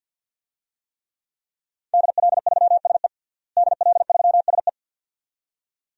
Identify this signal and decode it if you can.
Morse code, heard as a keyed tone, about 52 wpm, 700 Hz. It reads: DL3SE DL3SE
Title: DXCC-100 Call Signs - 5 Characters - Repeated 2x - 50wpm